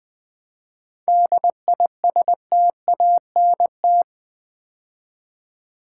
Morse code, heard as a keyed tone, about 20 wpm, 700 Hz. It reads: DISTANT
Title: 7 Letter Words - First 3 Letters Given - 20wpm